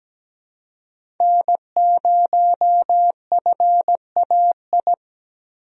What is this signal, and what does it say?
Morse code, heard as a keyed tone, about 17 wpm, 700 Hz. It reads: N0FAI
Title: USA Call Signs - 5 Characters - Repeated 3x - 17wpm